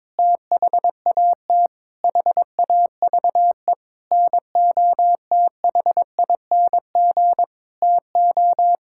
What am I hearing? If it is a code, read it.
Morse code, heard as a keyed tone, about 22 wpm, 700 Hz. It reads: THAT HAVE NOTHING TO
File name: practice / 22wpm / Sentences with 8-Word Phrases - 22wpm